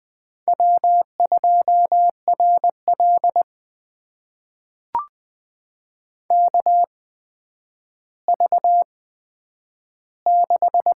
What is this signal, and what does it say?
Morse code, heard as a keyed tone, about 21 wpm, spaced as for 10 wpm, 700 Hz. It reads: W2RL KV6
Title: USA Call Signs - 4 Characters - 8x Character Spacing - 20wpm